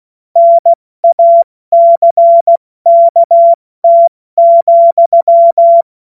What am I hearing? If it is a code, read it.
Morse code, heard as a keyed tone, about 16 wpm, 680 Hz. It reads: NACKT,
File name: Morsekurs / Witze / Witz 18. Januar 2022, 70 F80